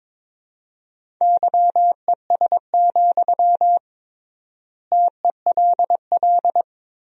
Morse code, heard as a keyed tone, about 22 wpm, 700 Hz. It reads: YES, TELL
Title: 3x Word-Spacing - Sentences from Top 500 Words - 22wpm